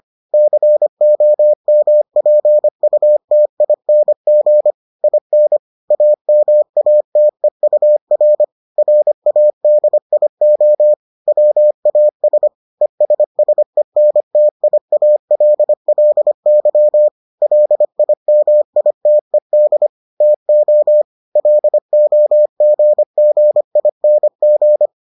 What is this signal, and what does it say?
Morse code, heard as a keyed tone, about 25 wpm, 600 Hz. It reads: COMPUTING IN AMATEUR RADIO WAS ESSENTIALLY LIMITED TO LOGGING